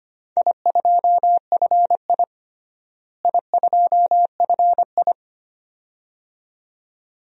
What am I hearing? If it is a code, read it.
Morse code, heard as a keyed tone, about 25 wpm, 700 Hz. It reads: I2FI I2FI
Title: DXCC-100 Call Signs - 4 Characters - Repeated 2x - 25wpm